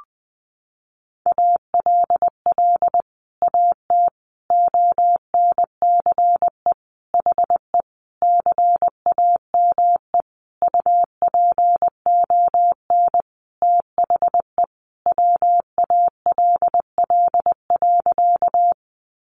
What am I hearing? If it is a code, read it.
Morse code, heard as a keyed tone, about 20 wpm, 700 Hz. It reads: ALL AT ONCE HE CAME UPON THE WALL.
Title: Sentences from Top 1100 Words 20wpm